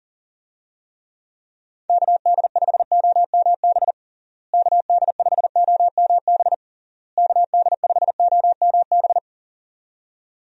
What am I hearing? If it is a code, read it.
Morse code, heard as a keyed tone, about 40 wpm, 700 Hz. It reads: KD5OMB KD5OMB KD5OMB
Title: USA Call Signs - 6 Characters - Repeated 3x - 40wpm